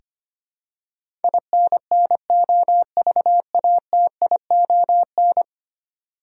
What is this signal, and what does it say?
Morse code, heard as a keyed tone, about 25 wpm, 700 Hz. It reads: INNOVATION
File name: 10 Letter Words - First 3 Letters Given - 25wpm